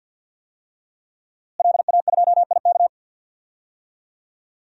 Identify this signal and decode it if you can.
Morse code, heard as a keyed tone, about 46 wpm, 700 Hz. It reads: LA2IK
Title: DXCC-100 Call Signs - 5 Characters - Repeated 2x - 50wpm